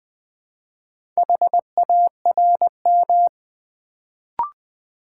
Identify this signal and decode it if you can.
Morse code, heard as a keyed tone, about 20 wpm, 700 Hz. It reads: HARM
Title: Build A Word - 1 Letter At A Time - 4-Letter Words - 20wpm